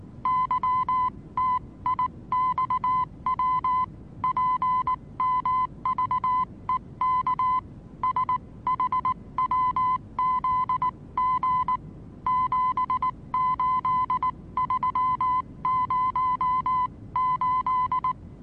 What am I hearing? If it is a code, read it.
Morse code, heard as a keyed tone, about 19 wpm, 1010 Hz. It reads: YTIXW PMVEK SHWZG 78308
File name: repos / Morsekurs / DJ4UF Morsen26